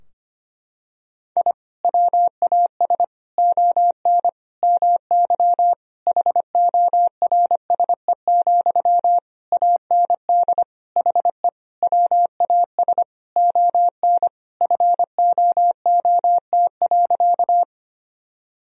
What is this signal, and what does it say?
Morse code, heard as a keyed tone, about 25 wpm, 700 Hz. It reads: I WAS ON MY HORSE, AND HE WAS ON FOOT.
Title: Sentences from Top 1000 Words - Review Part 3 - 25wpm